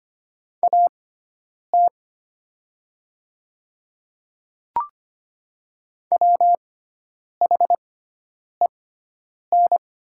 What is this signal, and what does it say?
Morse code, heard as a keyed tone, about 25 wpm, spaced as for 9 wpm, 700 Hz. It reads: AT WHEN
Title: 6x Character Spacing - Top 100 Words - 25wpm